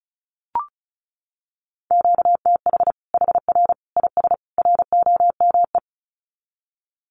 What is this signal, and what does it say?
Morse code, heard as a keyed tone, about 34 wpm, 700 Hz. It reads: QTH HR IS ROME
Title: Common QSO Phrases 35wpm